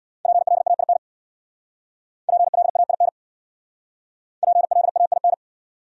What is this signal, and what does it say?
Morse code, heard as a keyed tone, about 67 wpm, 700 Hz. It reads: PLAIN PLAIN PLAIN